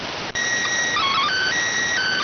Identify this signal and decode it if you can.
SSTV (slow-scan television), PD 290.